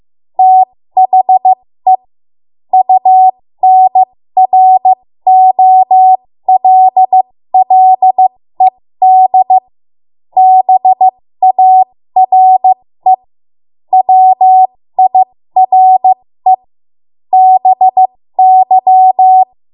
Morse code, heard as a keyed tone, about 15 wpm, spaced as for 13 wpm, 750 Hz. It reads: THE UNROLLED BARE WIRE BY
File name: Morse / 13WPM